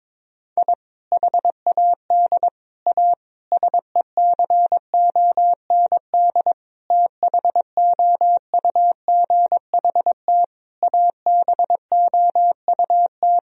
Morse code, heard as a keyed tone, about 22 wpm, 700 Hz. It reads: I HAD A SECOND THOUGHT ABOUT
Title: Sentences from Top 1000 Words - Review Part 1 - 22wpm